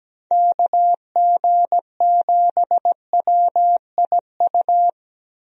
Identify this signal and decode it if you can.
Morse code, heard as a keyed tone, about 17 wpm, 700 Hz. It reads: KG7WIU
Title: USA Call Signs - 6 Characters - 17wpm